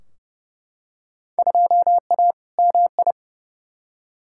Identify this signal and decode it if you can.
Morse code, heard as a keyed tone, about 31 wpm, 700 Hz. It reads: 2A MI
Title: Field Day Exchanges Encore - 30wpm